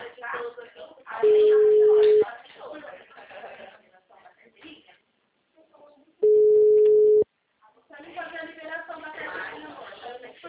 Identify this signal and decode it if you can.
Ringback tone (Europe, 425 Hz), with speech.
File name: out- -2015- - - .10993